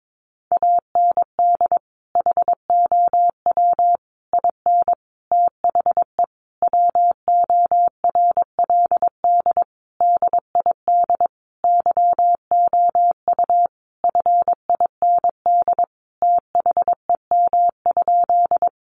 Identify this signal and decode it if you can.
Morse code, heard as a keyed tone, about 22 wpm, 700 Hz. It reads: AND HOW IN THE WORLD DID YOU FIND THEM?
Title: Sentences from Top 200 Words 22wpm